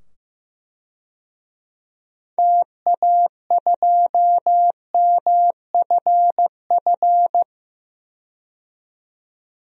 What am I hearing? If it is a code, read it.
Morse code, heard as a keyed tone, about 15 wpm, 700 Hz. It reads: TA2MFF
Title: DXCC-100 Call Signs - 6 Characters - 15wpm